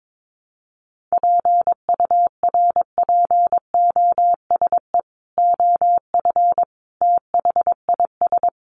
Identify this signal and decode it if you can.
Morse code, heard as a keyed tone, about 22 wpm, 700 Hz. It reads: PURPOSE OF THIS
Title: Sets of 5 Words 22wpm